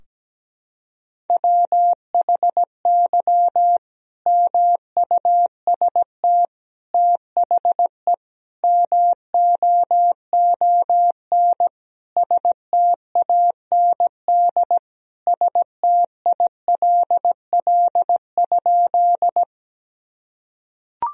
Morse code, heard as a keyed tone, about 17 wpm, 700 Hz. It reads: WHY MUST THE MOON STAND STILL?
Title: Sentences from Top 1000 Words - Review Part 3 - 17wpm